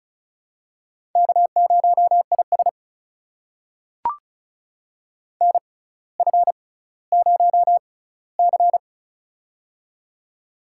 Morse code, heard as a keyed tone, about 36 wpm, spaced as for 18 wpm, 700 Hz. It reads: K0IS NF0C
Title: USA Call Signs - 4 Characters - 6x Character Spacing - 35wpm